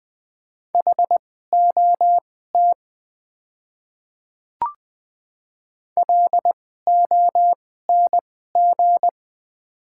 Morse code, heard as a keyed tone, about 20 wpm, spaced as for 15 wpm, 700 Hz. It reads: HOT LONG
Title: 2x Character Spacing - Top 200 Words - 20wpm